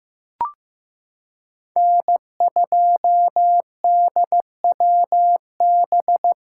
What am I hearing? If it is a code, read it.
Morse code, heard as a keyed tone, about 15 wpm, 700 Hz. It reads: N2DWB
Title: USA Callsign DE Callsign - 5 Characters - 15wpm